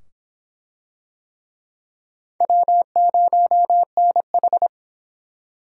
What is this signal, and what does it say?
Morse code, heard as a keyed tone, about 23 wpm, 700 Hz. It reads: W0NH